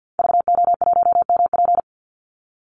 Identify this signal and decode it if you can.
Morse code, heard as a keyed tone, about 50 wpm, 700 Hz. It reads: VO1MP
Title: Call Signs Repeated 2x - 50wpm